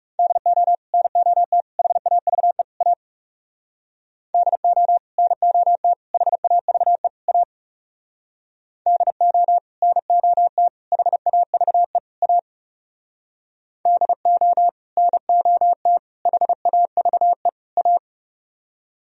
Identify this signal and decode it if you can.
Morse code, heard as a keyed tone, about 38 wpm, 700 Hz. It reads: DO NOT HAVE A DO NOT HAVE A DO NOT HAVE A DO NOT HAVE A